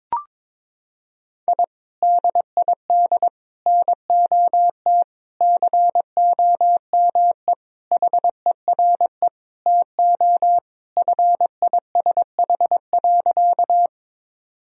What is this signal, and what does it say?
Morse code, heard as a keyed tone, about 22 wpm, 700 Hz. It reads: I DID NOT COME HERE TO FISH.
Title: Sentences from Top 1000 Words - Review Part 1 - 22wpm